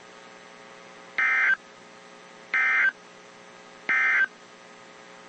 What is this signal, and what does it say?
Emergency Alert System (SAME) tones, heard as harsh buzzing bursts. It is heard from 1.1 to 4.4 s.
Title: WZ2523 - NOAA Weather Radio